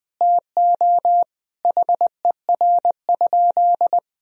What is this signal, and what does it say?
Morse code, heard as a keyed tone, about 20 wpm, 700 Hz. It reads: TO HER?
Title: Sentences from Top 200 Words and Abbreviations 20wpm